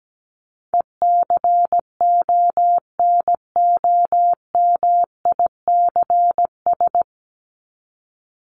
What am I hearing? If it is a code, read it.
Morse code, heard as a keyed tone, about 17 wpm, 700 Hz. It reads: ECONOMICS
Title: Top 501-750 Words 17wpm